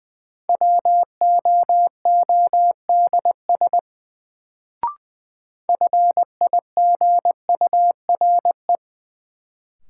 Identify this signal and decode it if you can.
Morse code, heard as a keyed tone, about 20 wpm, 700 Hz. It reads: WOODS FIGURE
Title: Sentences from Top 1000 Words - Review Part 2 - New Word Drill - 20wpm